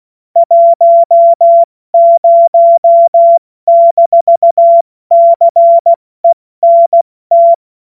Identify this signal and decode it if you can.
Morse code, heard as a keyed tone, about 16 wpm, 680 Hz. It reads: 10-CENT
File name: Morsekurs / Witze / Witz 11. Mai 2021, 70 F80